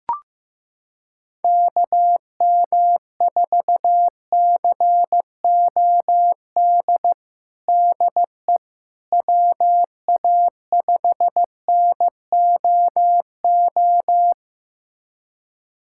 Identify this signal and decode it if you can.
Morse code, heard as a keyed tone, about 16 wpm, 700 Hz. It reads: KM4COD DE WA5NOO
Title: USA Callsign DE Callsign - 6 Characters - 15wpm